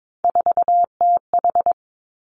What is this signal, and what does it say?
Morse code, heard as a keyed tone, about 22 wpm, 700 Hz. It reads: 4TH
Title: Three Challenging Letter-Number Combinations 22wpm